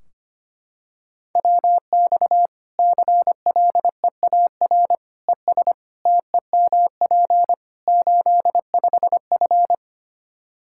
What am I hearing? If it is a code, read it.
Morse code, heard as a keyed tone, about 25 wpm, 700 Hz. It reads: WX CLEAR ES TEMP 85F
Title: Practice QSO - Exchange 2 - Info - 25wpm